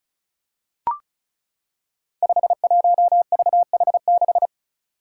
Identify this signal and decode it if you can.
Morse code, heard as a keyed tone, about 36 wpm, 700 Hz. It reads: 51VH6
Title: Five Challenging Letter-Number Combinations 35wpm